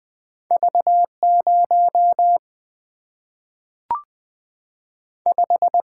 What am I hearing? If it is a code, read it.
Morse code, heard as a keyed tone, about 20 wpm, 700 Hz. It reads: V0 5
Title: Single-Letter Single-Number 20wpm